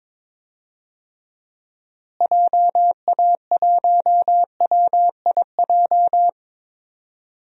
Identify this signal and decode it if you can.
Morse code, heard as a keyed tone, about 22 wpm, 700 Hz. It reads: JA1WIJ
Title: DXCC-100 Call Signs - 6 Characters - Repeated 2x - 22wpm